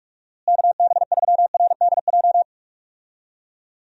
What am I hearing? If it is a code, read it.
Morse code, heard as a keyed tone, about 46 wpm, 700 Hz. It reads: KB3RDJ
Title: USA Call Signs - 6 Characters - Repeated 2x - 45wpm